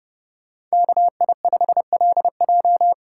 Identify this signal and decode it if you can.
Morse code, heard as a keyed tone, about 37 wpm, 700 Hz. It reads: KI5LJ
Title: Speed Racing - USA 2x2 Call Signs 30wpm